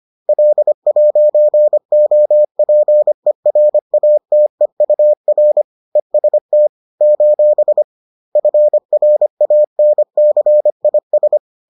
Morse code, heard as a keyed tone, about 25 wpm, 600 Hz. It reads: L'OPERATEUR EST : FRANCIS
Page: bulletin UFT sous forme texte et en CW